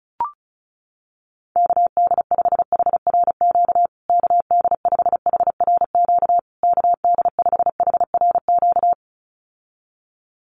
Morse code, heard as a keyed tone, about 34 wpm, 700 Hz. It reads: KD5HRQ KD5HRQ KD5HRQ
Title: USA Call Signs Repeated 3x 35wpm